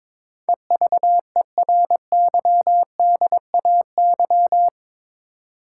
Morse code, heard as a keyed tone, about 22 wpm, 700 Hz. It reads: EVERYDAY
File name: practice / 22wpm / Eight Letter Words - 22wpm